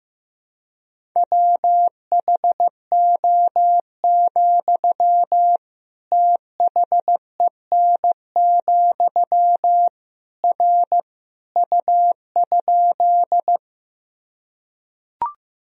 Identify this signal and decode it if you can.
Morse code, heard as a keyed tone, about 15 wpm, 700 Hz. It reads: WHO, THEN, R U?
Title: Sentences from Top 100 Words and Abbreviations 15wpm